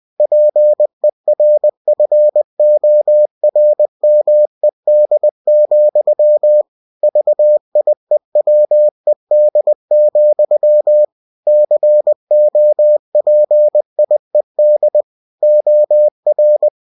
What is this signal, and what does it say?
Morse code, heard as a keyed tone, about 20 wpm, 600 Hz. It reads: PERFORMED, VIEWED, COPIED OR
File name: winnie-20wpm